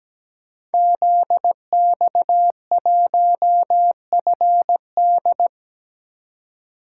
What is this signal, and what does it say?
Morse code, heard as a keyed tone, about 17 wpm, 700 Hz. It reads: ZX1FD
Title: Code Groups - 5 Letter:Numbers - 17wpm